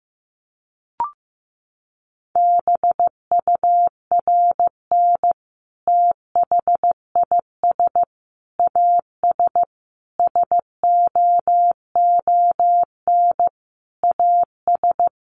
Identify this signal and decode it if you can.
Morse code, heard as a keyed tone, about 15 wpm, 700 Hz. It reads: BURN THIS AS SOON AS
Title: Sentences from Top 1300 Words 15wpm